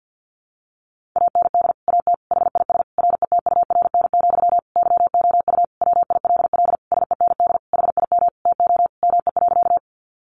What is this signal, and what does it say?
Morse code, heard as a keyed tone, about 50 wpm, 700 Hz. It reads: AND AT HIS RETURN, YOU WILL SEND HIM TO ME.
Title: Sentences from Top 1500 Words 50wpm